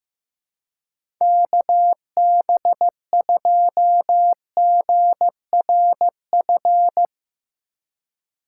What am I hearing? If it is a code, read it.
Morse code, heard as a keyed tone, about 15 wpm, 700 Hz. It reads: KB2GRF
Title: USA Call Signs - 6 Characters - 4x Character Spacing - 15wpm